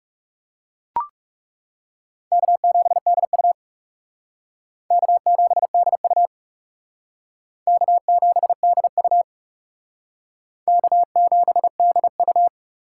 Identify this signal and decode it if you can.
Morse code, heard as a keyed tone, about 39 wpm, 690 Hz. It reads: K7DU K7DU K7DU K7DU